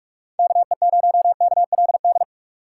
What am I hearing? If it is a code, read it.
Morse code, heard as a keyed tone, about 45 wpm, 700 Hz. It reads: KE0KLD
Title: USA Call Signs - 6 Characters - 3x Character Spacing - 45wpm